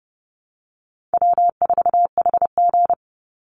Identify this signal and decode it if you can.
Morse code, heard as a keyed tone, about 30 wpm, 700 Hz. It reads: W4HG
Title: Easy Call Signs Repeated 3x - 30wpm